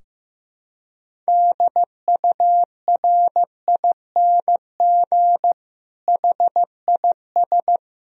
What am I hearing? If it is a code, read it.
Morse code, heard as a keyed tone, about 15 wpm, 700 Hz. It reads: DURING HIS
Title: Sentences from Top 1000 Words - Review Part 2 - 15wpm